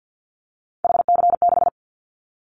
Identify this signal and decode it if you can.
Morse code, heard as a keyed tone, about 50 wpm, 700 Hz. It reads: HC6
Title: Callsign Prefixes for Top 100 DXCC - 50wpm